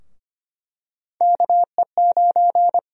Morse code, heard as a keyed tone, about 25 wpm, 700 Hz. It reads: KE9
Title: USA Call Sign Prefixes - Repeated 2x - 25wpm